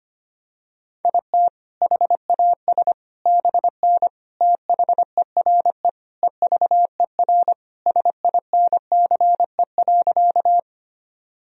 Morse code, heard as a keyed tone, about 25 wpm, 700 Hz. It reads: IT HAS BN THERE EVER SINCE.
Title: Sentences from Top 400 Words and Abbreviations 25wpm